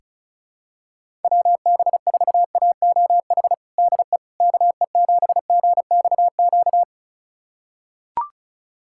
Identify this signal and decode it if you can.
Morse code, heard as a keyed tone, about 34 wpm, 700 Hz. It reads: WB4AOH DE KE7GXQ
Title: USA Callsign DE Callsign 35wpm